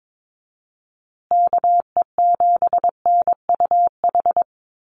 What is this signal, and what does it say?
Morse code, heard as a keyed tone, about 22 wpm, 700 Hz. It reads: KE7NUH
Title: USA Call Signs Repeated 3x 22wpm